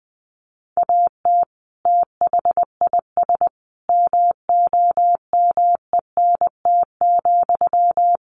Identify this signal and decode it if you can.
Morse code, heard as a keyed tone, about 20 wpm, 700 Hz. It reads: AT THIS MOMENT,
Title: Sentences from Top 1000 Words 20wpm